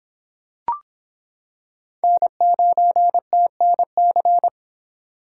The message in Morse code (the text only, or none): N9TNC